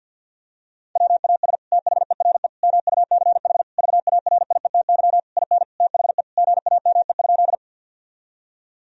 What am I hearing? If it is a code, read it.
Morse code, heard as a keyed tone, about 50 wpm, 700 Hz. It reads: WAS THERE MUCH VARIETY IN THE GAME?
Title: Sentences from Top 1000 Words - Review Part 4 - 50wpm